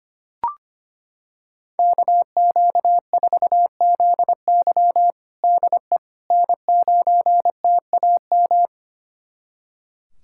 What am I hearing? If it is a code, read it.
Morse code, heard as a keyed tone, about 25 wpm, 700 Hz. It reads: KQ4ZY DE N9TAM
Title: USA Callsign DE Callsign - 5 Characters - 25wpm